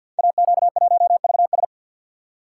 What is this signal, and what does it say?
Morse code, heard as a keyed tone, about 53 wpm, 700 Hz. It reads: AQ1VS
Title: Code Groups - 5 Letter:Numbers - 50wpm